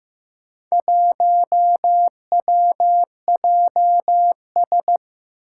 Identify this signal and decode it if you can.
Morse code, heard as a keyed tone, about 15 wpm, 700 Hz. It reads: 1WJS
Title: Four Challenging Letter-Number Combinations 15wpm